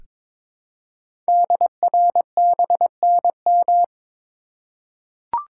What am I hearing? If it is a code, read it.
Morse code, heard as a keyed tone, about 23 wpm, 700 Hz. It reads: DRBNM
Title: Code Groups - 5 Letters - 22wpm